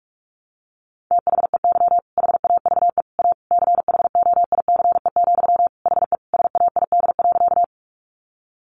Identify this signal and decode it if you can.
Morse code, heard as a keyed tone, about 44 wpm, 700 Hz. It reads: THEY HAVE A CHOICE, HE SAID.
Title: Sentences from Top 900 Words 45wpm